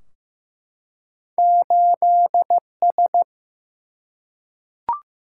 Morse code, heard as a keyed tone, about 15 wpm, 700 Hz. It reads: 8S
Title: Single-Letter Single-Number 15wpm